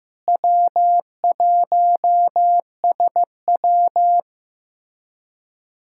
Morse code, heard as a keyed tone, about 16 wpm, 700 Hz. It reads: W1SW
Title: USA Callsign DE Callsign - 4 Characters - 15wpm